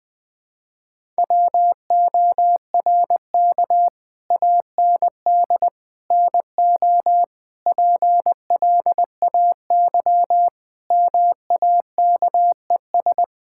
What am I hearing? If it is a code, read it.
Morse code, heard as a keyed tone, about 21 wpm, 700 Hz. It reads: WORK AND NO PLAY MAKES
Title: Sentences with 8-Word Phrases - Spoken Beginning - 20wpm